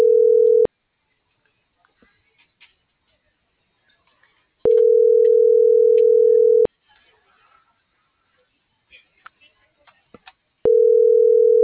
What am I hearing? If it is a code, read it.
Ringback tone (North America), 440 + 480 Hz.